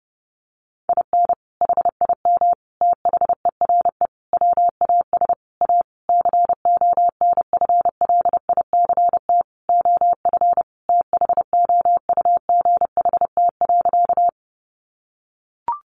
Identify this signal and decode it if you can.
Morse code, heard as a keyed tone, about 30 wpm, 700 Hz. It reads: IN HIM THERE WAS A CONFLICT OF THOUGHT.
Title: Sentences from Top 1300 Words 30wpm